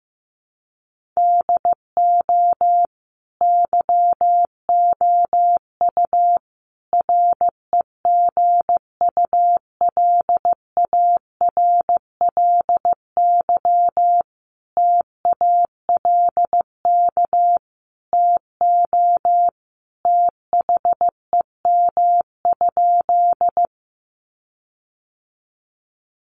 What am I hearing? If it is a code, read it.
Morse code, heard as a keyed tone, about 15 wpm, 700 Hz. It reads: DO YOU REGULARLY TALK TO THEM?
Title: Sentences from Top 1400 Words 15wpm